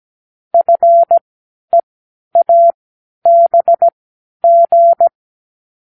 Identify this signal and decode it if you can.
Morse code, heard as a keyed tone, about 17 wpm, spaced as for 10 wpm, 680 Hz. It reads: FEABG
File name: k7qo / Lesson12(A-H)